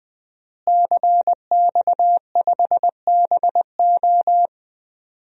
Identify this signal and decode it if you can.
Morse code, heard as a keyed tone, about 20 wpm, 700 Hz. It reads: CX5BO
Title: DXCC-100 Call Signs - 5 Characters - Repeated 3x - 20wpm